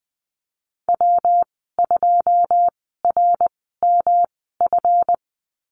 Morse code, heard as a keyed tone, about 20 wpm, spaced as for 15 wpm, 700 Hz. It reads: W2RMF
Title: 147 Lesson 24g - Letter y - Call Signs 20wpm